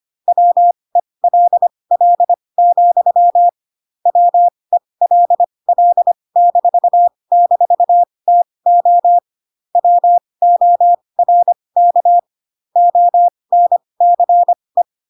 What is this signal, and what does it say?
Morse code, heard as a keyed tone, about 25 wpm, spaced as for 20 wpm, 700 Hz. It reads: WELL, WELL--TO WORK ONCE
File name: Pinocchio / Pinocchio ch1 15 25 ik1hge